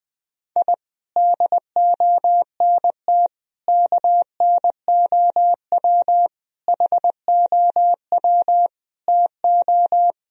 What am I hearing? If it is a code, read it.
Morse code, heard as a keyed tone, about 20 wpm, 700 Hz. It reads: I DONT KNOW HOW TO
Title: Sentences from Top 1000 Words - Review Part 1 - 20wpm